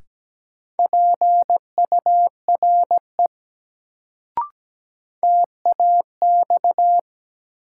Morse code, heard as a keyed tone, about 17 wpm, 700 Hz. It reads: PURE TAX
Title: Sentences from Top 1500 Words - New Word Drill Alt - 17wpm